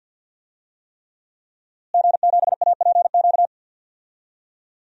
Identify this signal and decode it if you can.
Morse code, heard as a keyed tone, about 46 wpm, 700 Hz. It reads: G7APQ
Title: DXCC-100 Call Signs - 5 Characters - Repeated 3x - 50wpm